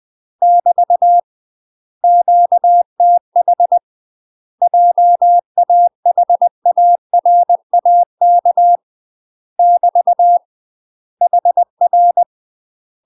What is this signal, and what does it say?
Morse code, heard as a keyed tone, about 20 wpm, 700 Hz. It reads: <BT> QTH JAHARAK <BT> HR
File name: UK MASTER QSOs TRAINING / 20WPM / UK MASTER QSOs 0062